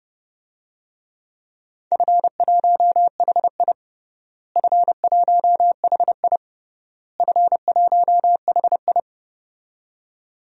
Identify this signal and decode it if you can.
Morse code, heard as a keyed tone, about 29 wpm, 700 Hz. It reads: F1HI F1HI F1HI
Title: DXCC-100 Call Signs - 4 Characters - Repeated 3x - 30wpm